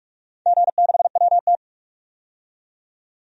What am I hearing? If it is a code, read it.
Morse code, heard as a keyed tone, about 46 wpm, 700 Hz. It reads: G6WT
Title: DXCC-100 Call Signs - 4 Characters - 45wpm